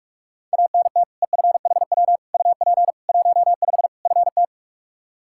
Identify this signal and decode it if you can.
Morse code, heard as a keyed tone, about 46 wpm, 700 Hz. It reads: ANT EFHW UP 15 FT
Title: Practice QSO - Exchange 2 - Info - 45wpm